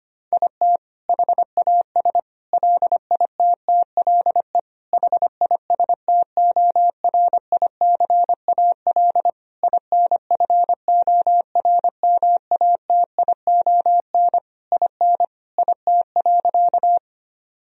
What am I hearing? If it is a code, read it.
Morse code, heard as a keyed tone, about 25 wpm, 700 Hz. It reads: IT HAS LITTLE HISTORICAL INFORMATION IN IT.
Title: Sentences from Top 1000 Words - Review Part 4 - 25wpm